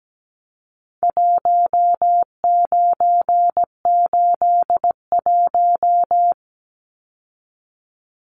Morse code, heard as a keyed tone, about 17 wpm, 700 Hz. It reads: 1981
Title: Numbers - Contest Style - 4-Digit 17wpm